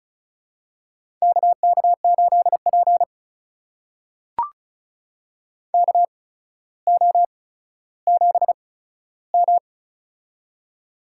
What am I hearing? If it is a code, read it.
Morse code, heard as a keyed tone, about 36 wpm, spaced as for 15 wpm, 700 Hz. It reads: KK8P KO7M
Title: USA Call Signs - 4 Characters - 8x Character Spacing - 35wpm